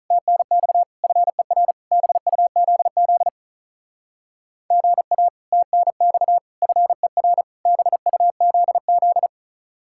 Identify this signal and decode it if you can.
Morse code, heard as a keyed tone, about 43 wpm, 700 Hz. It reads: TNX FER BUZZ GA TNX FER BUZZ
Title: Speed-Racing Common QSO Phrases - 35wpm